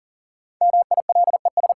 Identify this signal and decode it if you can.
Morse code, heard as a keyed tone, about 40 wpm, 700 Hz. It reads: MILES
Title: Sentences from Top 1000 Words - Review Part 3 - New Word Drill - 40wpm